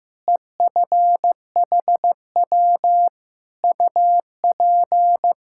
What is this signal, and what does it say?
Morse code, heard as a keyed tone, about 15 wpm, 700 Hz. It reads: EFHW UP
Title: Practice QSO - Exchange 2 - Info - 15wpm